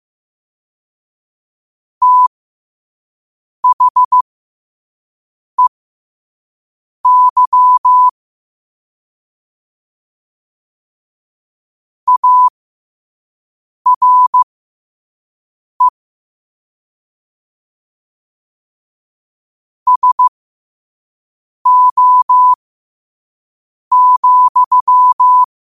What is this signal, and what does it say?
Morse code, heard as a keyed tone, about 15 wpm, spaced as for 5 wpm, 1000 Hz. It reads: THEY ARE SO,
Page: New quotes every day in morse code at 5 Words per minute.